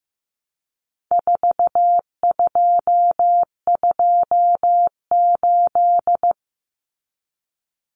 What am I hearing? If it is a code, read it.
Morse code, heard as a keyed tone, about 15 wpm, 700 Hz. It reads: 4228
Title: Numbers - 4-Digit 15wpm